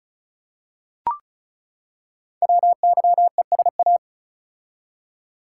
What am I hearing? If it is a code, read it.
Morse code, heard as a keyed tone, about 36 wpm, 700 Hz. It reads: WYESA
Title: Code Groups - 5 Letters - 35wpm